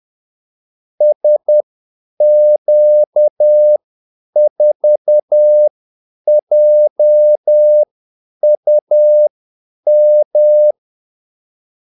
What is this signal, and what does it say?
Morse code, heard as a keyed tone, about 10 wpm, spaced as for 8 wpm, 600 Hz. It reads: SQ4JUM
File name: Callsigns-10-8wpm